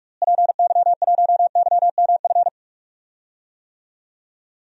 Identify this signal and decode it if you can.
Morse code, heard as a keyed tone, about 47 wpm, 700 Hz. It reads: PY1YMF
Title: DXCC-100 Call Signs - 6 Characters - Repeated 2x - 45wpm